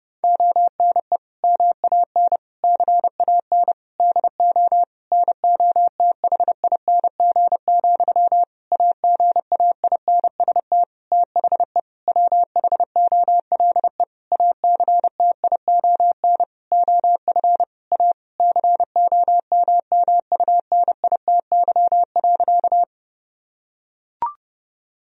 Morse code, heard as a keyed tone, about 30 wpm, 700 Hz. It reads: ONE MAN CAN DO NOTHING, AGAINST THE WHOLE ACTION OF A COMMUNITY.
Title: Sentences from Top 1000 Words - Review Part 4 - 30wpm